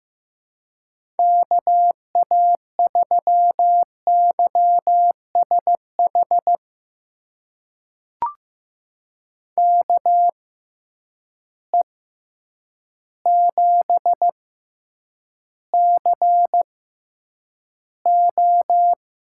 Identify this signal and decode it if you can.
Morse code, heard as a keyed tone, about 15 wpm, spaced as for 8 wpm, 700 Hz. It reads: KA3YSH KE7CO
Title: USA Call Signs - 6 Characters - 6x Character Spacing - 15wpm